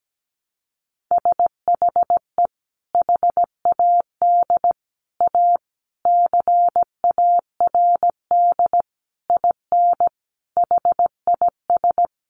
Morse code, heard as a keyed tone, about 17 wpm, 700 Hz. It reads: SHE HAD A CARD IN HIS
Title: Sentences from Top 600 Words 17wpm